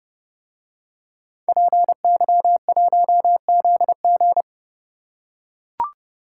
Morse code, heard as a keyed tone, about 30 wpm, 700 Hz. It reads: PY1ZG